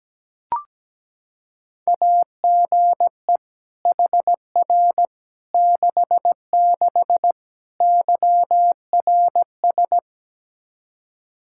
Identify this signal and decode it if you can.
Morse code, heard as a keyed tone, about 17 wpm, 700 Hz. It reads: AGE HR 66 YRS
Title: Practice QSO - Exchange 3 - More Info - 17wpm